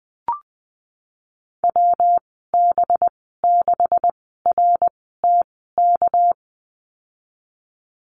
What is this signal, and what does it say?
Morse code, heard as a keyed tone, about 20 wpm, spaced as for 15 wpm, 700 Hz. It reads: WB6RTK
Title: 207 Lesson 33g - Letter j - Call Signs 20wpm